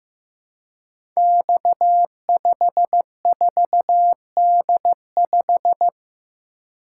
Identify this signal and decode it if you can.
Morse code, heard as a keyed tone, about 15 wpm, 700 Hz. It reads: X54D5
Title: Code Groups - 5 Letter:Numbers - 15wpm